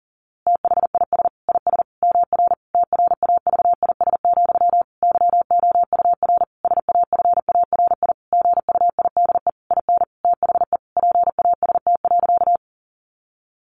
Morse code, heard as a keyed tone, about 40 wpm, 700 Hz. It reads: THIS IS MR TRAVIS, YOUR SAFARI GUIDE IN THE PAST.
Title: A Sound Of Thunder 40wpm